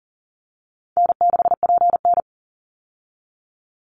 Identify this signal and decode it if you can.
Morse code, heard as a keyed tone, about 40 wpm, 700 Hz. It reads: N6PN
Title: Easy Call Signs Repeated 2x - 40wpm